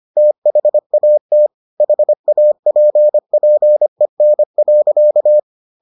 Morse code, heard as a keyed tone, about 25 wpm, 600 Hz. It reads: THAT HAPPEN.